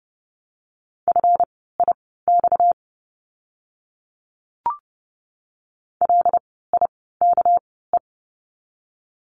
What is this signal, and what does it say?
Morse code, heard as a keyed tone, about 30 wpm, spaced as for 17 wpm, 700 Hz. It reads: FIX LIKE
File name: course / 30:10wpm / 230 Lesson 37f - Letter x - Words 30wpm